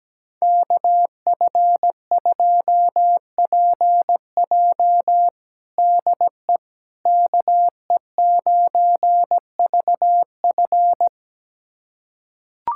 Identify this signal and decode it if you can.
Morse code, heard as a keyed tone, about 17 wpm, 700 Hz. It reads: KF2PJ DE KE9VF
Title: USA Callsign DE Callsign - 5 Characters - 17wpm